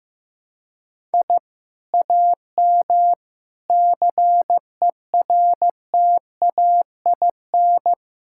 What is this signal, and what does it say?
Morse code, heard as a keyed tone, about 15 wpm, 700 Hz. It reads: I AM CERTAIN
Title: Sentences from Top 1000 Words - Review Part 6 - 15wpm